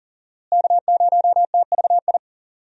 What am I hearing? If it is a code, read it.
Morse code, heard as a keyed tone, about 40 wpm, 700 Hz. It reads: K0TVI
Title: USA Call Signs - 5 Characters - Repeated 3x - 40wpm